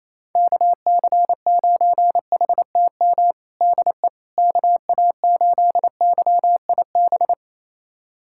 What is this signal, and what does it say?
Morse code, heard as a keyed tone, about 28 wpm, 700 Hz. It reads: KC9HTM DE KA8YIB
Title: USA Callsign DE Callsign 28wpm